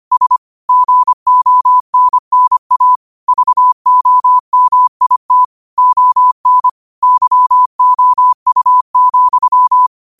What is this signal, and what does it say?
Morse code, heard as a keyed tone, about 25 wpm, 1000 Hz. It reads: S GONNA VOMIT ON YOU,
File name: qotdfiles / 25wpm